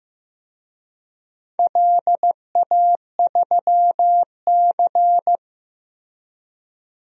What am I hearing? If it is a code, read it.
Morse code, heard as a keyed tone, about 15 wpm, 700 Hz. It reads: LA3C
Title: DXCC-100 Call Signs - 4 Characters - Repeated 2x - 15wpm